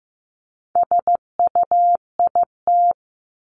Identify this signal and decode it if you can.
Morse code, heard as a keyed tone, about 15 wpm, 700 Hz. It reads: SUIT
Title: Top 1501-1750 Words 15wpm